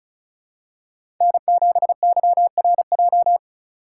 Morse code, heard as a keyed tone, about 34 wpm, 700 Hz. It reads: N7YRJ
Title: USA Call Signs - 5 Characters - 2x Character Spacing - 35wpm